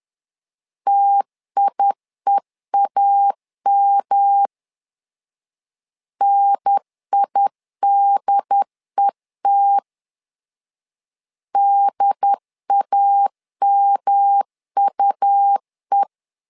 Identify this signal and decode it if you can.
Morse code, heard as a keyed tone, about 11 wpm, 800 Hz. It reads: TIEAM NIDET DAMUE
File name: LEZIONI / 3 MWRO / 3 M 10wpm 1828